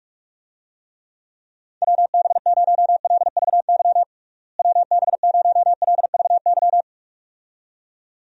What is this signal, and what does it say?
Morse code, heard as a keyed tone, about 41 wpm, 700 Hz. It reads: WB0LVY WB0LVY
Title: USA Call Signs - 6 Characters - Repeated 2x - 45wpm